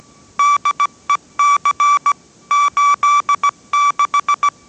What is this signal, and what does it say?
Morse code, heard as a keyed tone, about 17 wpm, 1160 Hz. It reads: DEC 86